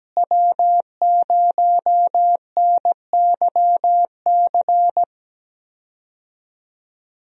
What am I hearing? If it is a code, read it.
Morse code, heard as a keyed tone, about 17 wpm, 700 Hz. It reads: W0NYC
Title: USA Call Signs - 5 Characters - Repeated 2x - 17wpm